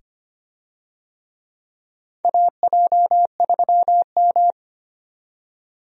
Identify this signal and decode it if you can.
Morse code, heard as a keyed tone, about 25 wpm, 700 Hz. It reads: AJ3M
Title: USA Call Signs - 4 Characters - 25wpm